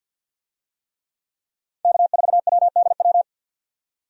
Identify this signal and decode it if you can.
Morse code, heard as a keyed tone, about 51 wpm, 700 Hz. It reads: K4WDW
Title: USA Call Signs - 5 Characters - 50wpm